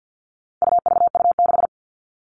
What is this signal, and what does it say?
Morse code, heard as a keyed tone, about 50 wpm, 700 Hz. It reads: UVU6
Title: All Cut Numbers - 4-Digit 50wpm